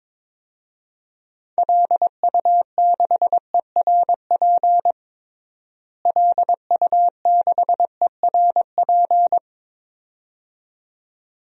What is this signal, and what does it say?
Morse code, heard as a keyed tone, about 22 wpm, 700 Hz. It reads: LU6ERP LU6ERP
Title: DXCC-100 Call Signs - 6 Characters - Repeated 2x - 22wpm